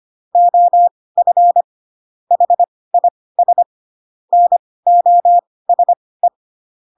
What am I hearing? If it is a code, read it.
Morse code, heard as a keyed tone, about 25 wpm, spaced as for 18 wpm, 700 Hz. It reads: OF HIS NOSE
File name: Pinocchio / Pinocchio ch1 12 25 ik1hge